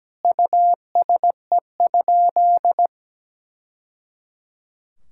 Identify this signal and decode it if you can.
Morse code, heard as a keyed tone, about 17 wpm, 700 Hz. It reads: USE?
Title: Sentences from Top 1000 Words - Review Part 6 - 17wpm